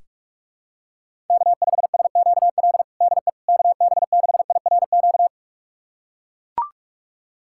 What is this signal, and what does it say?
Morse code, heard as a keyed tone, about 43 wpm, 700 Hz. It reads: K5SQL DE KB6IRQ
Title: USA Callsign DE Callsign 45wpm